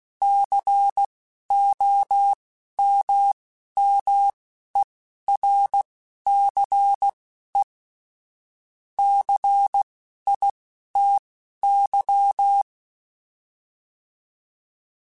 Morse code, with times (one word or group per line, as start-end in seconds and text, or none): 0.2-7.6 COMMERCE
9.0-12.6 CITY